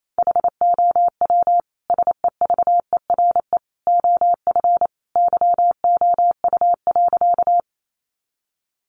Morse code, heard as a keyed tone, about 28 wpm, 700 Hz. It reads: HOW SEVERE OF YOU.
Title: Sentences from Top 1400 Words 28wpm